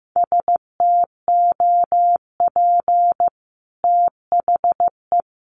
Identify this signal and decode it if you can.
Morse code, heard as a keyed tone, about 15 wpm, 700 Hz. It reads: STOP THE
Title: Sentences from Top 1200 Words 15wpm